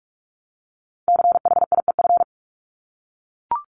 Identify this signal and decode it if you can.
Morse code, heard as a keyed tone, about 44 wpm, 700 Hz. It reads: CHIEF
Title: LY Suffix - Top 100 Words with Base Words 45wpm